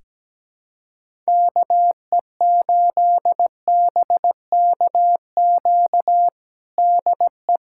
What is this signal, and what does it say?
Morse code, heard as a keyed tone, about 17 wpm, 700 Hz. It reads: KE8BKQ DE
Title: USA Callsign DE Callsign - 6 Characters - 17wpm